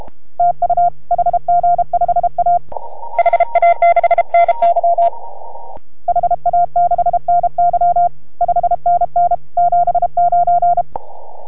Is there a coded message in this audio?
Yes: Morse code from 0.4 to 10.8 s, heard as a keyed tone, about 32 wpm, 690 Hz. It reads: TU HG5A HA6NY HA6NY 5NN 79